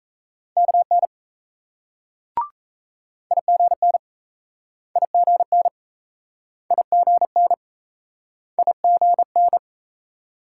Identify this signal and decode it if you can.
Morse code, heard as a keyed tone, about 38 wpm, 700 Hz. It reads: KN IGN IGN IGN IGN